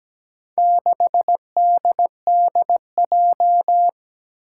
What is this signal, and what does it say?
Morse code, heard as a keyed tone, about 17 wpm, 700 Hz. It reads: 6DDJ
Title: Four Challenging Letter-Number Combinations 17wpm